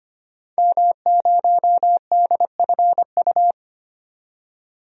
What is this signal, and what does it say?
Morse code, heard as a keyed tone, about 25 wpm, 700 Hz. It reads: M0DFU
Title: DXCC-100 Call Signs - 5 Characters - 25wpm